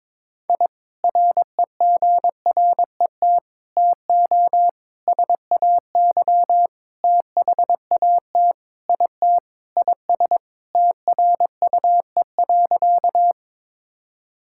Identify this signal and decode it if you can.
Morse code, heard as a keyed tone, about 22 wpm, 700 Hz. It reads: I REGRET TO SAY THAT IT IS TRUE.
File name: practice / 22wpm / Sentences with 3-Word Phrases - 22wpm